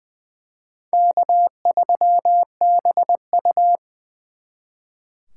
Morse code, heard as a keyed tone, about 24 wpm, 700 Hz. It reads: K3BU
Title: Speed Racing - USA Call Signs 20wpm